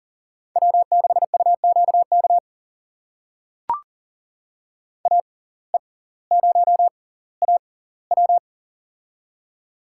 Morse code, heard as a keyed tone, about 37 wpm, spaced as for 20 wpm, 700 Hz. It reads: W6UQK AE0AW
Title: USA Call Signs - 5 Characters - 6x Character Spacing - 40wpm